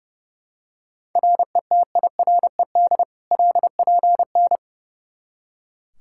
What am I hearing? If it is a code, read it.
Morse code, heard as a keyed tone, about 29 wpm, 700 Hz. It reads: RETIRED LPN
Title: Practice QSO - Exchange 3 - More Info - 30wpm